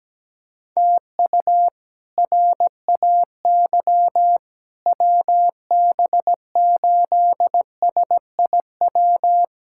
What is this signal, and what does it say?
Morse code, heard as a keyed tone, about 17 wpm, 700 Hz. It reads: TU RAY WB8SIW
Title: K1USN SST QSO Practice - 17wpm